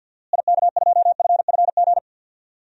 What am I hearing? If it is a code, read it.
Morse code, heard as a keyed tone, about 53 wpm, 700 Hz. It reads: IK2FFG
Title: DXCC-100 Call Signs - 6 Characters - Repeated 2x - 50wpm